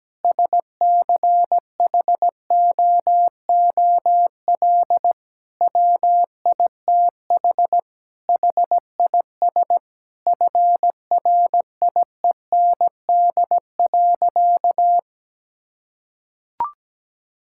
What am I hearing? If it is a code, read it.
Morse code, heard as a keyed tone, about 17 wpm, 700 Hz. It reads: SCHOOL WITH HIS FRIEND.
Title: Sentences from Top 1600 Words 17wpm